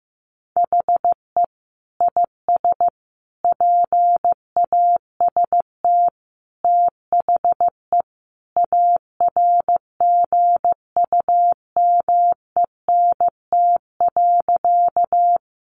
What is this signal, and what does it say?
Morse code, heard as a keyed tone, about 15 wpm, 700 Hz. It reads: HE IS PAST THE ARGUMENT.
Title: Sentences from Top 1400 Words 15wpm